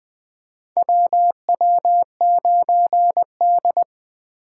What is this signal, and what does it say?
Morse code, heard as a keyed tone, about 20 wpm, 700 Hz. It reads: WW9D
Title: USA Call Signs - 4x Character Spacing - 20wpm